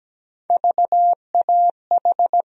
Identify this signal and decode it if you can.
Morse code, heard as a keyed tone, about 17 wpm, 700 Hz. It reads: VAH
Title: USA Call Sign Postfixes - 3 Letters 17wpm